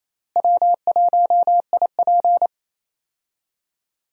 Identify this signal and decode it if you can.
Morse code, heard as a keyed tone, about 28 wpm, 700 Hz. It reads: W1IP
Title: USA Call Signs - 4 Characters - Repeated 2x - 28wpm